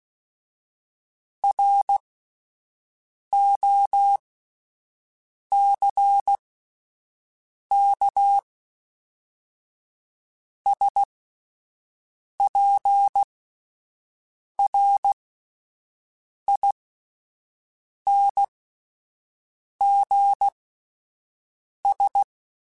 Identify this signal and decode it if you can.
Morse code, heard as a keyed tone, about 16 wpm, spaced as for 5 wpm, 800 Hz. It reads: ROCKSPRINGS